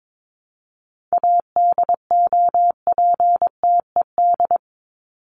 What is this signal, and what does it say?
Morse code, heard as a keyed tone, about 22 wpm, 700 Hz. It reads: ADOPTED
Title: ED Suffix - Top 100 Words with Base Words 22wpm